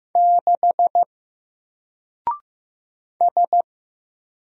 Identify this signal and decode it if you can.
Morse code, heard as a keyed tone, about 15 wpm, 700 Hz. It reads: 6 S
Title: Single Letter-Number Alt 15wpm